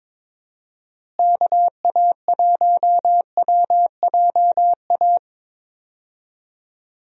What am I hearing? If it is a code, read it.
Morse code, heard as a keyed tone, about 22 wpm, 700 Hz. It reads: KA1WJA
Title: USA Call Signs - 6 Characters - Repeated 3x - 22wpm